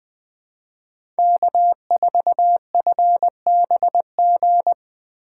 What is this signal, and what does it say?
Morse code, heard as a keyed tone, about 20 wpm, 700 Hz. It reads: K4FBG
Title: USA Call Signs - 5 Characters - 3x Character Spacing - 20wpm